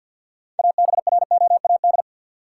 Morse code, heard as a keyed tone, about 51 wpm, 700 Hz. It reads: ABROAD
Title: Six Letter Words - 50wpm